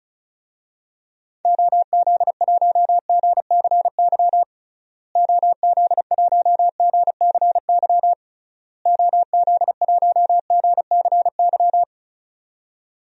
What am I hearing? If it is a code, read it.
Morse code, heard as a keyed tone, about 36 wpm, 700 Hz. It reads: OZ1GCY OZ1GCY OZ1GCY
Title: DXCC-100 Call Signs - 6 Characters - Repeated 3x - 35wpm